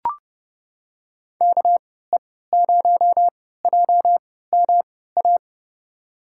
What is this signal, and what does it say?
Morse code, heard as a keyed tone, about 30 wpm, spaced as for 17 wpm, 700 Hz. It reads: KE0JMA
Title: USA Call Signs - 6 Characters - 3x Character Spacing - 30wpm